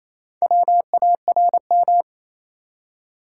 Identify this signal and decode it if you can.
Morse code, heard as a keyed tone, about 28 wpm, 700 Hz. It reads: WARM
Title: Four Letter Words - 28wpm